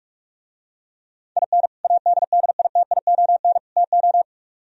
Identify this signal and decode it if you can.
Morse code, heard as a keyed tone, about 49 wpm, 700 Hz. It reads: IN ADDITION TO